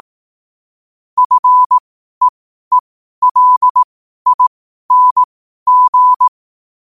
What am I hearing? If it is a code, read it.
Morse code, heard as a keyed tone, about 18 wpm, spaced as for 12 wpm, 1000 Hz. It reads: FEELING
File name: qotdfiles / 12wpm